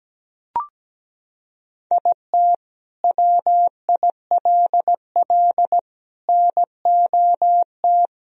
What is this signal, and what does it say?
Morse code, heard as a keyed tone, about 17 wpm, 700 Hz. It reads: IT WILL NOT
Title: Sentences from Top 1000 Words - Review Part 4 - 17wpm